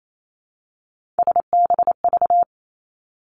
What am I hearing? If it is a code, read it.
Morse code, heard as a keyed tone, about 28 wpm, 700 Hz. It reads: SBV
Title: Three Challenging Letter-Number Combinations 28wpm